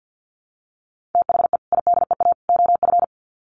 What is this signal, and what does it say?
Morse code, heard as a keyed tone, about 50 wpm, 700 Hz. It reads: THE IDEA OF
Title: Sets of 3 Words 50wpm